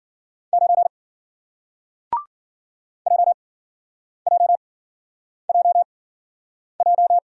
Morse code, heard as a keyed tone, about 52 wpm, spaced as for 9 wpm, 700 Hz. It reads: 9 JJJJ